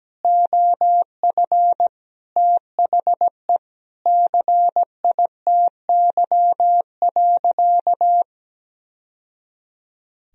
Morse code, heard as a keyed tone, about 17 wpm, 700 Hz. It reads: OF THE CITY.
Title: Sentences from Top 1000 Words - Review Part 1 - 17wpm